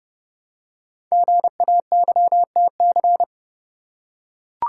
Morse code, heard as a keyed tone, about 30 wpm, 700 Hz. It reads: GAYTC
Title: Code Groups - 5 Letters - 30wpm